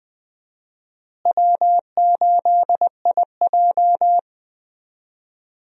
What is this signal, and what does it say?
Morse code, heard as a keyed tone, about 20 wpm, 700 Hz. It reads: W8IJ
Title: USA Call Signs - 4 Characters - Repeated 2x - 20wpm